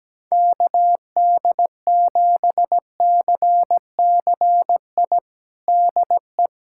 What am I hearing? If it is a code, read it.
Morse code, heard as a keyed tone, about 17 wpm, 700 Hz. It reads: KD7CCI DE
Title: USA Callsign DE Callsign - 6 Characters - 17wpm